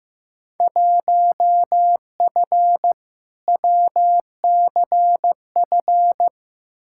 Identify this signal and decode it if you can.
Morse code, heard as a keyed tone, about 15 wpm, 700 Hz. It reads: 1F WCF
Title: Field Day Exchanges - 15wpm